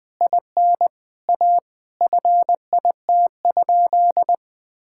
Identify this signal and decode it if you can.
Morse code, heard as a keyed tone, about 19 wpm, 700 Hz. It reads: IN A FIT?
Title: Sentences from Top 1000 Words - Review Part 5 - 20wpm